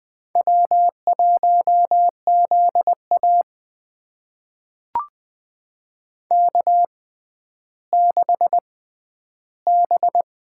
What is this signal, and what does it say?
Morse code, heard as a keyed tone, about 20 wpm, spaced as for 11 wpm, 700 Hz. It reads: W1ZA K6B
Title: USA Call Signs - 4 Characters - 6x Character Spacing - 20wpm